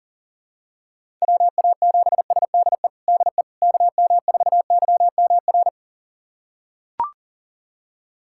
Morse code, heard as a keyed tone, about 40 wpm, 700 Hz. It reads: WA7SDE DE KM4YMR
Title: USA Callsign DE Callsign - 6 Characters - 40wpm